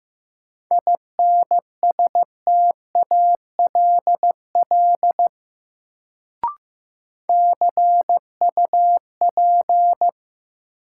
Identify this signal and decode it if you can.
Morse code, heard as a keyed tone, about 15 wpm, 700 Hz. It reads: INSTALL CUP
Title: Sentences from Top 1600 Words - New Word Drill - 15wpm